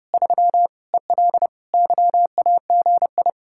Morse code, heard as a keyed tone, about 40 wpm, 700 Hz. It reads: 3 EL YAGI